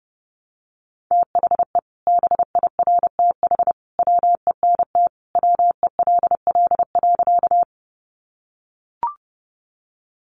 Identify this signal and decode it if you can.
Morse code, heard as a keyed tone, about 30 wpm, 700 Hz. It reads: THE BIRTH WENT WELL.
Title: Sentences from Top 800 Words 30wpm